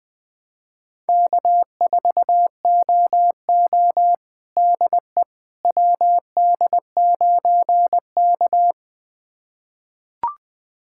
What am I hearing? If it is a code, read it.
Morse code, heard as a keyed tone, about 21 wpm, 700 Hz. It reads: K4OO DE WD9K
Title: USA Callsign DE Callsign - 4 Characters - 20wpm